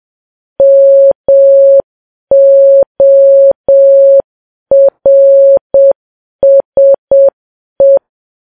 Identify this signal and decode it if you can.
Morse code, heard as a keyed tone, about 7 wpm, 550 Hz. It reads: MORSE
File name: Files Web tls / Morse on9fv / morse7kps